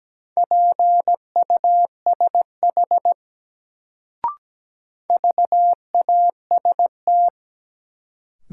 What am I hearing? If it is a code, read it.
Morse code, heard as a keyed tone, about 17 wpm, 700 Hz. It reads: PUSH VAST
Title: Sentences from Top 1300 Words - New Word Drill - 17wpm